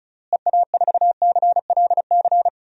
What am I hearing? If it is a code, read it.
Morse code, heard as a keyed tone, about 36 wpm, 700 Hz. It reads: EA4CLC
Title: DXCC-100 Call Signs - 6 Characters - Repeated 2x - 35wpm